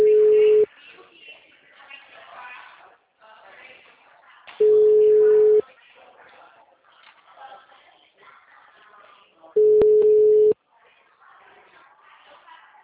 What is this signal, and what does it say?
Ringback tone (Europe, 425 Hz), with speech.
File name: out- -2019- - - .38297